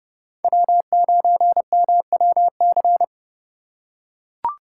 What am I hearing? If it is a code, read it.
Morse code, heard as a keyed tone, about 31 wpm, 700 Hz. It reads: W9MWC